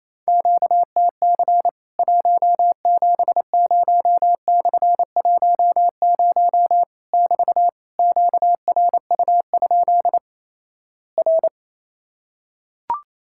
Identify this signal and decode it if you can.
Morse code, heard as a keyed tone, about 28 wpm, 700 Hz. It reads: QTC 170/10 <BT> QRU? R
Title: Worked All Europe DX Contest - QTC Exchange - 28wpm